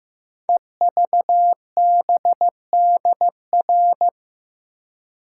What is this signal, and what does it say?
Morse code, heard as a keyed tone, about 15 wpm, 700 Hz. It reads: EVBDR
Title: Code Groups - 5 Letter:Numbers - 15wpm